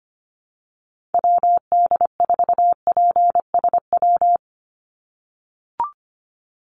Morse code, heard as a keyed tone, about 25 wpm, 700 Hz. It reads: WD4PSW
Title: USA Call Signs 25wpm